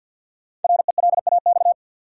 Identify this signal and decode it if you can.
Morse code, heard as a keyed tone, about 52 wpm, 700 Hz. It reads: RELAX
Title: Sentences from Top 1400 Words - New Word Drill - 50wpm